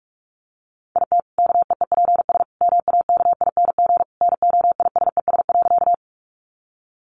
Morse code, heard as a keyed tone, about 44 wpm, 700 Hz. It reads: IT KEEPS MAKING NOISES.
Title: Additional Easy Phrases 45wpm